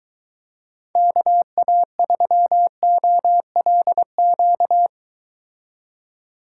Morse code, heard as a keyed tone, about 23 wpm, 700 Hz. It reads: KA3OLQ